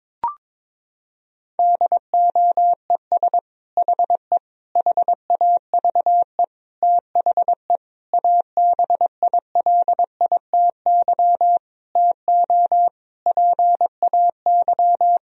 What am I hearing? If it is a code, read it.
Morse code, heard as a keyed tone, about 21 wpm, 700 Hz. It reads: DOES HE HAVE THE ABILITY TO PAY
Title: Sentences from Top 1000 Words - Review Part 3 - 22wpm